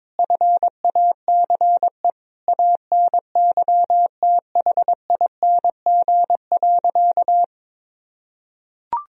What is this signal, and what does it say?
Morse code, heard as a keyed tone, about 22 wpm, 700 Hz. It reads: FACE ANYTHING.
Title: Sentences from Top 1000 Words - Review Part 2 - 22wpm